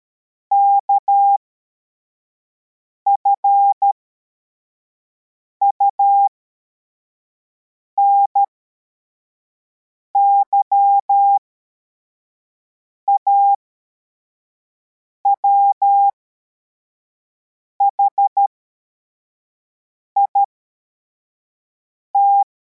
Morse code, heard as a keyed tone, about 13 wpm, spaced as for 4 wpm, 800 Hz. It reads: KFUNY AWHIT